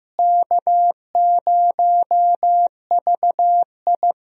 Morse code, heard as a keyed tone, about 15 wpm, 700 Hz. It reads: K0VI
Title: USA Call Signs - 4 Characters - Repeated 3x - 15wpm